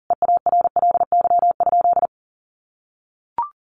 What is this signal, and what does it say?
Morse code, heard as a keyed tone, about 37 wpm, 700 Hz. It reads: EARLY?
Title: Sentences from Top 400 Words 40wpm